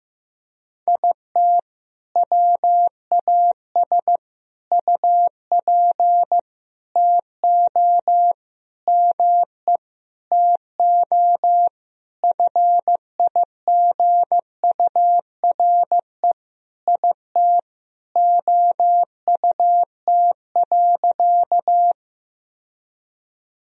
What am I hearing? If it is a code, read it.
Morse code, heard as a keyed tone, about 15 wpm, 700 Hz. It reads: IT WAS UP TO ME TO FIGURE IT OUT.
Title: Sentences from Top 1000 Words - Review Part 2 - 15wpm